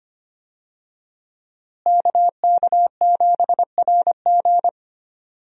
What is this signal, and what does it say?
Morse code, heard as a keyed tone, about 25 wpm, 700 Hz. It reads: KK7RG
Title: USA Call Signs - 5 Characters - Repeated 3x - 25wpm